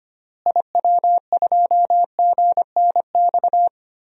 Morse code, heard as a keyed tone, about 25 wpm, 700 Hz. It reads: IW2GNX
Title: DXCC-100 Call Signs - 6 Characters - Repeated 3x - 25wpm